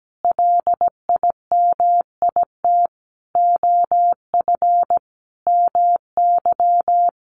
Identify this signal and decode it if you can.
Morse code, heard as a keyed tone, about 17 wpm, 700 Hz. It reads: LIMIT OF MY
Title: Sentences from Top 1500 Words 17wpm